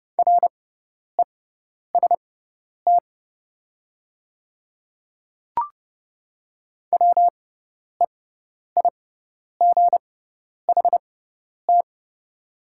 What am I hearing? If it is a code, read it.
Morse code, heard as a keyed tone, about 30 wpm, spaced as for 10 wpm, 700 Hz. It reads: REST WEIGHT
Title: 6x Character Spacing - Top 500 Words - 30wpm